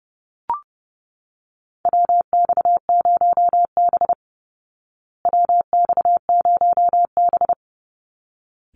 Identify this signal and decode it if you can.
Morse code, heard as a keyed tone, about 30 wpm, 700 Hz. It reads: WX0B WX0B
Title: Call Signs Repeated 3x - 30wpm